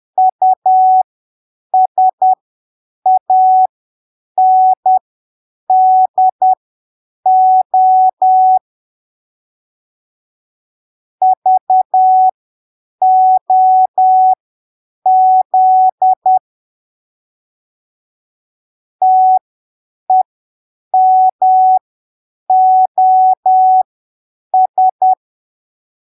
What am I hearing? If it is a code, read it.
Morse code, heard as a keyed tone, about 10 wpm, spaced as for 7 wpm, 750 Hz. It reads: USANDO VOZ TEMOS
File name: arquivos / texto / texto 04 10ppm